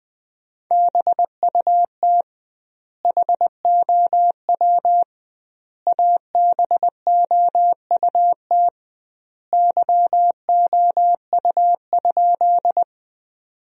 Morse code, heard as a keyed tone, about 20 wpm, 700 Hz. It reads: BUT HOW ABOUT YOU?
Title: 2x Word-Spacing - Sentences from Top 100 Words - 20wpm